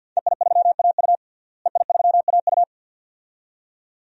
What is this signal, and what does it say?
Morse code, heard as a keyed tone, about 52 wpm, 700 Hz. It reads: EI3AU EI3AU
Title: DXCC-100 Call Signs - 5 Characters - Repeated 2x - 50wpm